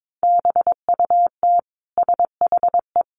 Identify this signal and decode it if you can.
Morse code, heard as a keyed tone, about 22 wpm, 700 Hz. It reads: BUT SHE
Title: Sentences from Top 1100 Words 22wpm